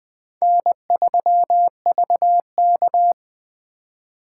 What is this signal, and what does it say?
Morse code, heard as a keyed tone, about 24 wpm, 700 Hz. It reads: N3VK
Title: Speed Racing - USA Call Signs 20wpm